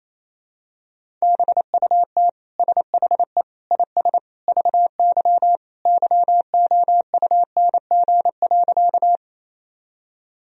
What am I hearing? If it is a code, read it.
Morse code, heard as a keyed tone, about 27 wpm, 700 Hz. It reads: BUT SHE IS VY YOUNG.
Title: Sentences from Top 400 Words and Abbreviations 28wpm